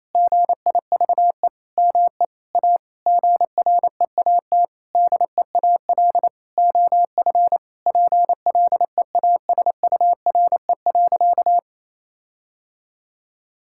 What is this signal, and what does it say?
Morse code, heard as a keyed tone, about 28 wpm, 700 Hz. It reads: GIVE ME A GREAT DEAL OF PLEASURE.
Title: Sentences from Top 1600 Words 28wpm